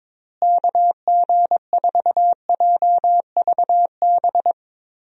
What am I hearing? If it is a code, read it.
Morse code, heard as a keyed tone, about 22 wpm, 700 Hz. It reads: KG4JVB
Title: USA Call Signs - 6 Characters - Repeated 2x - 22wpm